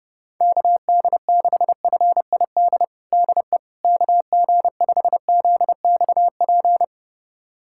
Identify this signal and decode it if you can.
Morse code, heard as a keyed tone, about 30 wpm, 700 Hz. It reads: KD6FID DE KG5ZXP
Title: USA Callsign DE Callsign - 6 Characters - 30wpm